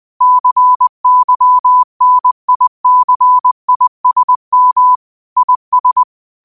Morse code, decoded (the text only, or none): CYNICISM IS